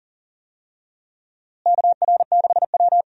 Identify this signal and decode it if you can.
Morse code, heard as a keyed tone, about 40 wpm, 700 Hz. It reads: KR6W
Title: USA Call Signs - 4 Characters - Repeated 2x - 40wpm